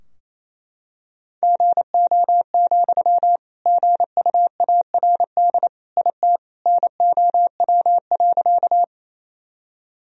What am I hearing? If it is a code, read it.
Morse code, heard as a keyed tone, about 28 wpm, 700 Hz. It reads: GO, GUARD IT NOW.
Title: Sentences from Top 1000 Words - Review Part 4 - 28wpm